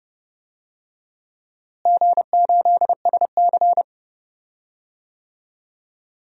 Morse code, heard as a keyed tone, about 30 wpm, 700 Hz. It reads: G8SC
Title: DXCC-100 Call Signs - 4 Characters - Repeated 3x - 30wpm